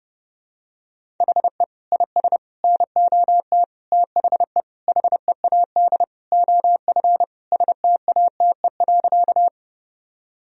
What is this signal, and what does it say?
Morse code, heard as a keyed tone, about 32 wpm, 700 Hz. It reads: HE IS NOT THE HEAD OF STATE.
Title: Sentences from Top 1000 Words - Review Part 2 - 30wpm